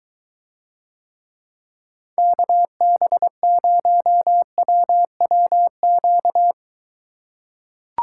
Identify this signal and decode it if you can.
Morse code, heard as a keyed tone, about 23 wpm, 700 Hz. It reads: KB0WWQ